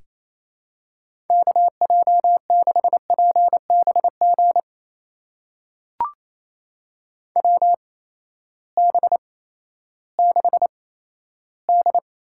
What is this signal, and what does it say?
Morse code, heard as a keyed tone, about 28 wpm, spaced as for 14 wpm, 700 Hz. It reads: KJ6PBG WB6D
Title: USA Call Signs - 8x Character Spacing - 28wpm